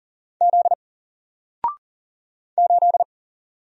Morse code, heard as a keyed tone, about 41 wpm, 700 Hz. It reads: Z 8
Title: Single Letter-Number Alt 40wpm